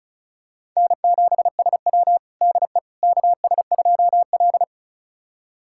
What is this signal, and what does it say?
Morse code, heard as a keyed tone, about 34 wpm, 700 Hz. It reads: N7SW DE KS2L
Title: USA Callsign DE Callsign - 4 Characters - 35wpm